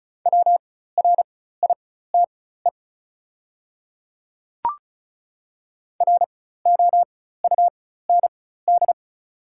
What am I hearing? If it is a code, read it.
Morse code, heard as a keyed tone, about 34 wpm, spaced as for 16 wpm, 700 Hz. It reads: WRITE ROUND
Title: 4x Character Spacing - Top 300 Words - 35wpm